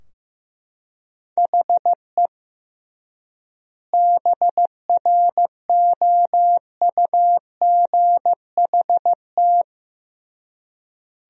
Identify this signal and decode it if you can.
Morse code, heard as a keyed tone, about 15 wpm, 700 Hz. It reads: HE BROUGHT
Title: 3x Word-Spacing - Sentences from Top 500 Words - 15wpm